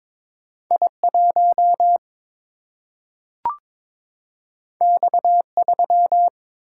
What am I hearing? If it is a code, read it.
Morse code, heard as a keyed tone, about 21 wpm, 700 Hz. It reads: I1 X3
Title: Single-Letter Single-Number 22wpm